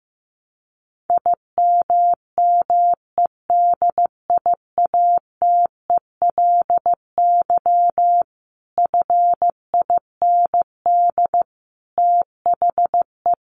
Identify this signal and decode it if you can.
Morse code, heard as a keyed tone, about 15 wpm, 700 Hz. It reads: IMMEDIATELY FIND THE
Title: Sentences from Top 1200 Words 15wpm